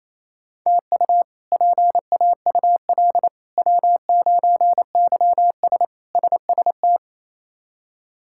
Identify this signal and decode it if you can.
Morse code, heard as a keyed tone, about 28 wpm, 700 Hz. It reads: TU PAUL W9YS SST
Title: K1USN SST QSO Practice - 28wpm